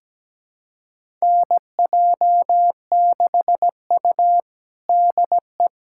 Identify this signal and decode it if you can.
Morse code, heard as a keyed tone, about 17 wpm, 700 Hz. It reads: NJ6U DE
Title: USA Callsign DE Callsign - 4 Characters - 17wpm